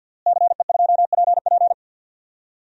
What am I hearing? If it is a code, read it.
Morse code, heard as a keyed tone, about 52 wpm, 700 Hz. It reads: CE2PP
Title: DXCC-100 Call Signs - 5 Characters - Repeated 3x - 50wpm